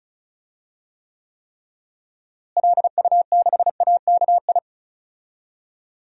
Morse code, heard as a keyed tone, about 36 wpm, 700 Hz. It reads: LU6AKI
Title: DXCC-100 Call Signs - 6 Characters - Repeated 3x - 35wpm